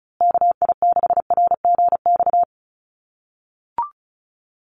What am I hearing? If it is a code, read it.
Morse code, heard as a keyed tone, about 34 wpm, 700 Hz. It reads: KI6RGX
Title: USA Call Signs Repeated 2x 35wpm